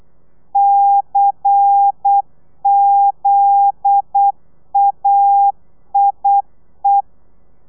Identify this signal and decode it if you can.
Morse code, heard as a keyed tone, about 8 wpm, 800 Hz. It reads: CZAIE
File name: telegrafia / i2tfj / Lez10 35